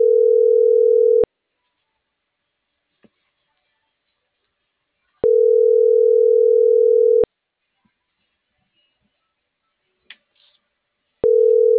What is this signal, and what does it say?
Ringback tone (North America), 440 + 480 Hz.